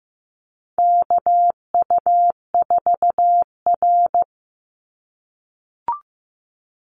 Morse code, heard as a keyed tone, about 15 wpm, 700 Hz. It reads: KU4R
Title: NA SOTA Chaser Callsigns 15wpm